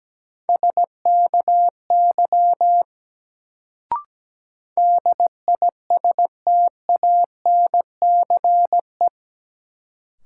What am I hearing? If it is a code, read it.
Morse code, heard as a keyed tone, about 17 wpm, 700 Hz. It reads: SKY DISTANCE
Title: Sentences from Top 1200 Words - New Word Drill - 17wpm